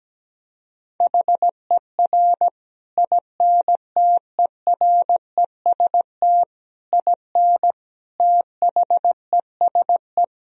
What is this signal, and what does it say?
Morse code, heard as a keyed tone, about 17 wpm, 700 Hz. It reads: HER INTEREST IN THESE
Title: Sentences from Top 1000 Words - Review Part 2 - 17wpm